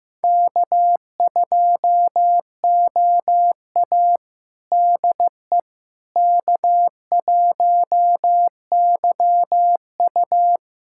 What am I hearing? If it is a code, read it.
Morse code, heard as a keyed tone, about 15 wpm, 700 Hz. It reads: K2OA DE K1YU
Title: USA Callsign DE Callsign - 4 Characters - 15wpm